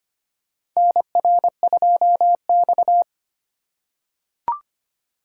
Morse code, heard as a keyed tone, about 25 wpm, 700 Hz. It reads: NR2X
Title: USA Call Signs - 4 Characters - 3x Character Spacing - 25wpm